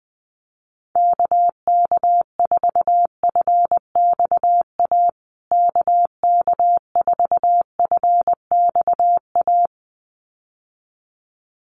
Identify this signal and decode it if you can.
Morse code, heard as a keyed tone, about 19 wpm, 700 Hz. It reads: KK4FXA KK4FXA
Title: USA Call Signs Repeated 2x 20wpm